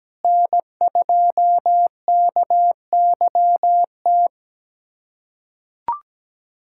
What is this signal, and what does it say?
Morse code, heard as a keyed tone, about 17 wpm, 700 Hz. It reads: N2KYT
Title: USA Callsign DE Callsign 17wpm